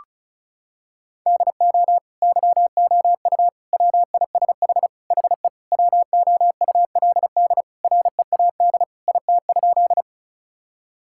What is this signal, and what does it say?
Morse code, heard as a keyed tone, about 36 wpm, 700 Hz. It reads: DO YOU WISH HE WOULD READ IT?
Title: Sentences from Top 1600 Words 35wpm